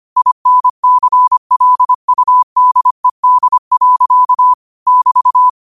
Morse code, heard as a keyed tone, about 25 wpm, 1000 Hz. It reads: INCLUDED. <BT>
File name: qotdfiles / 25wpm